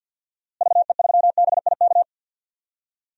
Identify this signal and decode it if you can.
Morse code, heard as a keyed tone, about 46 wpm, 700 Hz. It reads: VE3BIK
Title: DXCC-100 Call Signs - 6 Characters - Repeated 3x - 50wpm